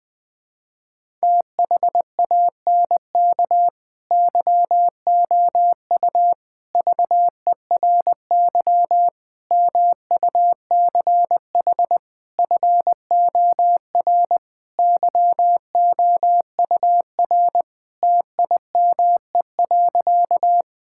Morse code, heard as a keyed tone, about 20 wpm, 700 Hz. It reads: THANK YOU VERY MUCH FOR YOUR TIME.
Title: Sentences with 5-Word Phrases - 20wpm